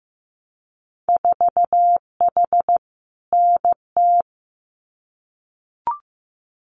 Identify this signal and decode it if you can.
Morse code, heard as a keyed tone, about 15 wpm, 700 Hz. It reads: 4H NT
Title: Winter Field Day Exchanges Encore 15wpm